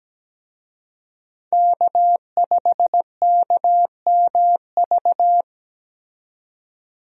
Morse code, heard as a keyed tone, about 17 wpm, 700 Hz. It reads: K5KMV
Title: USA Call Signs - 5 Characters - 17wpm